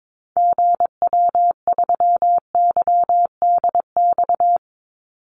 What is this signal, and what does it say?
Morse code, heard as a keyed tone, about 22 wpm, 700 Hz. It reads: GW3YDX
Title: Call Signs - 22wpm